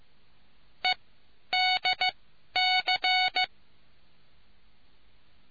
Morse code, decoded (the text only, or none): EDC